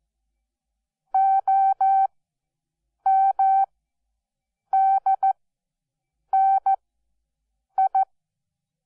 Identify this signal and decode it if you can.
Morse code, heard as a keyed tone, about 14 wpm, spaced as for 7 wpm, 770 Hz. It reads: OMDNI
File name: Onlinelehrgang / morsen / DJ4UF Morsen12